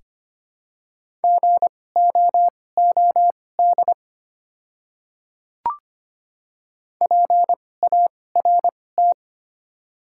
Morse code, heard as a keyed tone, about 25 wpm, spaced as for 18 wpm, 700 Hz. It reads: GOOD PART
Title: 2x Character Spacing - Top 300 Words - 25wpm